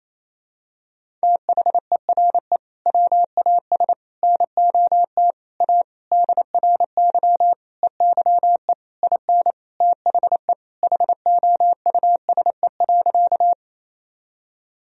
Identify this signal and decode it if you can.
Morse code, heard as a keyed tone, about 27 wpm, 700 Hz. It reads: THERE WAS NOT A DRY EYE IN THE HOUSE.
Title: Sentences from Top 1000 Words - Review Part 3 - 28wpm